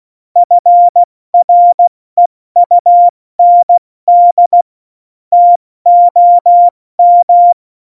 Morse code, heard as a keyed tone, about 16 wpm, 700 Hz. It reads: FREUND TOM
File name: Morsekurs / Witze / Witz 03. Oktober 17, Tempo 70 F80